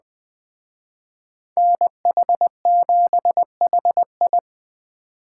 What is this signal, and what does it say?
Morse code, heard as a keyed tone, about 20 wpm, 700 Hz. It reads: NH7HI